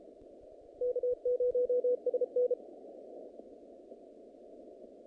Morse code, heard as a keyed tone, about 33 wpm, 500 Hz. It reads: K0SN
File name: contests / naqp-cw-k5tr-op-k5pi- -2200